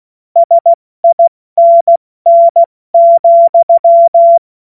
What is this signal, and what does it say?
Morse code, heard as a keyed tone, about 16 wpm, 680 Hz. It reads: SINN,
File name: Morsekurs / Witze / Witz 22. August 2023 70 F80